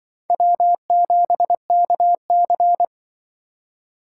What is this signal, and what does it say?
Morse code, heard as a keyed tone, about 25 wpm, 700 Hz. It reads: W7KC
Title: Speed Racing - USA 1x2 Call Signs 20wpm